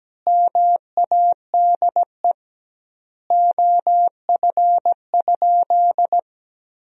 Morse code, heard as a keyed tone, about 17 wpm, 700 Hz. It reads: MADE OF?
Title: 2x Word-Spacing - Sentences from Top 200 Words - 17wpm